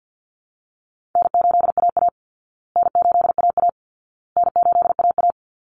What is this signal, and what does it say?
Morse code, heard as a keyed tone, about 50 wpm, 700 Hz. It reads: N8AA N8AA N8AA
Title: Easy Call Signs Repeated 3x - 50wpm